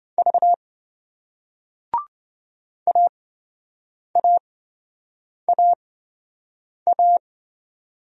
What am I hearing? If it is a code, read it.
Morse code, heard as a keyed tone, about 28 wpm, spaced as for 7 wpm, 700 Hz. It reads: V AAAA